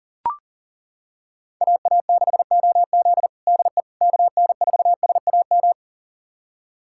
Morse code, heard as a keyed tone, about 40 wpm, 700 Hz. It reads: AA6OZ DE KN4SAM
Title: USA Callsign DE Callsign 40wpm